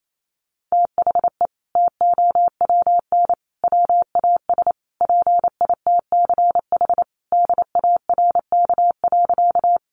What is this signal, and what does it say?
Morse code, heard as a keyed tone, about 28 wpm, 700 Hz. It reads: THE TOWN WAS PITCH DARK.
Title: Sentences from Top 1300 Words 28wpm